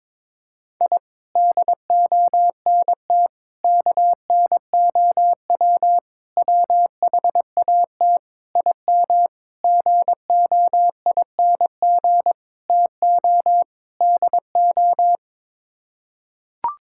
Morse code, heard as a keyed tone, about 22 wpm, 700 Hz. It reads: I DONT KNOW WHAT IM GOING TO DO
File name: practice / 22wpm / Sets of 8 Words - 22wpm